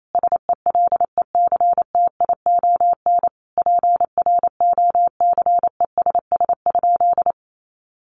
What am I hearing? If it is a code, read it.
Morse code, heard as a keyed tone, about 28 wpm, 700 Hz. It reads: SELECTION PROCESS?
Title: Sentences from Top 1400 Words 28wpm